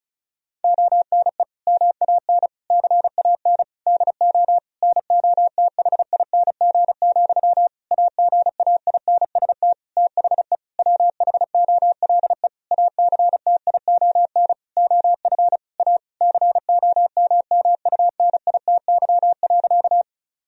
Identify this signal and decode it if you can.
Morse code, heard as a keyed tone, about 34 wpm, 700 Hz. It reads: ONE MAN CAN DO NOTHING, AGAINST THE WHOLE ACTION OF A COMMUNITY.
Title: Sentences from Top 1000 Words - Review Part 4 - 35wpm